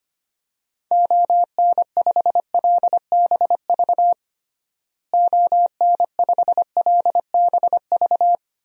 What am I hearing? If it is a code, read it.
Morse code, heard as a keyed tone, about 25 wpm, 700 Hz. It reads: ON5LBV ON5LBV
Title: DXCC-100 Call Signs - 6 Characters - Repeated 2x - 25wpm